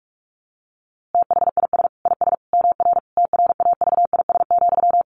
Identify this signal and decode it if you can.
Morse code, heard as a keyed tone, about 44 wpm, 700 Hz. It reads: THIS IS MR TRAVIS,
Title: A Sound Of Thunder 45wpm